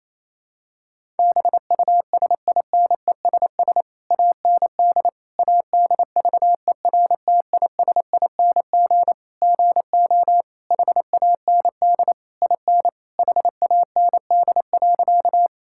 Morse code, heard as a keyed tone, about 28 wpm, 700 Hz. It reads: BUSINESS AND ADVERTISING GO HAND IN HAND.
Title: Sentences from Top 1000 Words - Review Part 6 - 28wpm